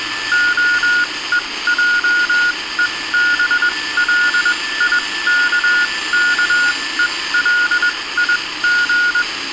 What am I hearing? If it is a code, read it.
Morse code, heard as a keyed tone, about 21 wpm, spaced as for 17 wpm, 1440 Hz. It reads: OEJEBLIKKELIG